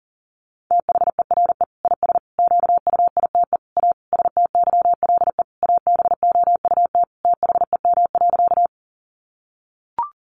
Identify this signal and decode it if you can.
Morse code, heard as a keyed tone, about 40 wpm, 700 Hz. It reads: THERE IS QUITE A STYLE ABOUT THEM.
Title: Sentences from Top 800 Words 40wpm